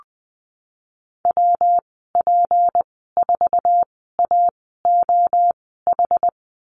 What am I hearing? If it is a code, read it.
Morse code, heard as a keyed tone, about 20 wpm, spaced as for 15 wpm, 700 Hz. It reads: WP4AOH
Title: 186 Lesson 30g - Letter b - Call Signs 20wpm